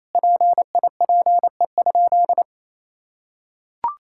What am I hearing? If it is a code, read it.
Morse code, heard as a keyed tone, about 28 wpm, 700 Hz. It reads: PIPE?
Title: Sentences from Top 1600 Words 28wpm